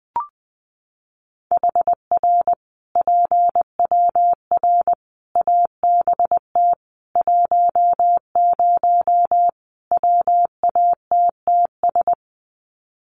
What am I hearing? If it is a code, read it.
Morse code, heard as a keyed tone, about 20 wpm, 700 Hz. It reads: HR PWR ABT 10 WATTS
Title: Common QSO Phrases 20wpm